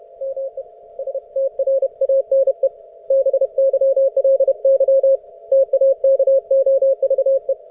Morse code, heard as a keyed tone, about 30 wpm, 540 Hz. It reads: ME STRANE BYLY TAKOVE